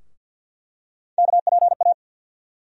Morse code, heard as a keyed tone, about 49 wpm, 700 Hz. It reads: KPA
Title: USA Call Sign Postfixes - 3 Letters 50wpm